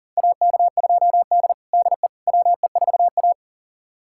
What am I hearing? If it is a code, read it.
Morse code, heard as a keyed tone, about 40 wpm, 700 Hz. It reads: AK2D DE WE4A
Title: USA Callsign DE Callsign - 4 Characters - 40wpm